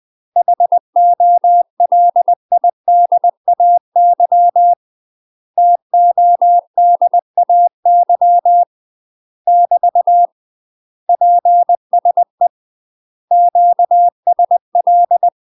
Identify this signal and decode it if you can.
Morse code, heard as a keyed tone, about 20 wpm, 700 Hz. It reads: HOLIDAY TODAY <BT> PSE QSL
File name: UK MASTER QSOs TRAINING / 20WPM / UK MASTER QSOs 0052